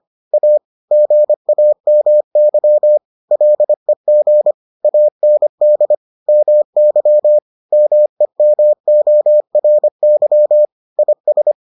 Morse code, heard as a keyed tone, about 25 wpm, 600 Hz. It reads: A GAMY LEG AND MY MEMORY IS